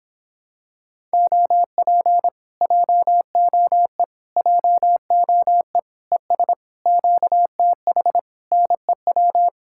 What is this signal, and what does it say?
Morse code, heard as a keyed tone, about 26 wpm, 700 Hz. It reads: OP JOE JOE ES QTH NEW